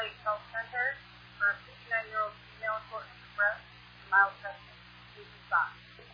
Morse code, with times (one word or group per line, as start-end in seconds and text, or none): none